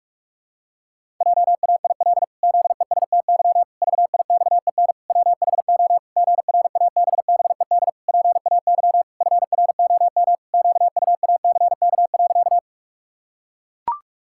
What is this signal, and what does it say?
Morse code, heard as a keyed tone, about 41 wpm, 700 Hz. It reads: JAIL ZESTY VIXEN WHO GRABBED PAY FROM QUACK.
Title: Pangram Challenge v2 - 45wpm